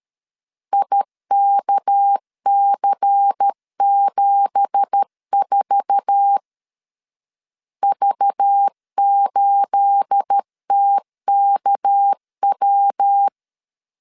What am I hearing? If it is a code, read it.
Morse code, heard as a keyed tone, about 13 wpm, 800 Hz. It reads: IKC74 V8TKW